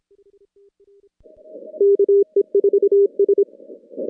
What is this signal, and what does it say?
Morse code, heard as a keyed tone, about 25 wpm, 400 Hz. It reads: KE4S
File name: contests / CW-SS-k5tr- -2200